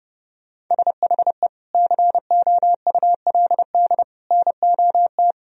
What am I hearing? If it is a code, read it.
Morse code, heard as a keyed tone, about 32 wpm, 700 Hz. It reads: SHE COULD NOT
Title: Sentences from Top 1000 Words - Review Part 4 - 30wpm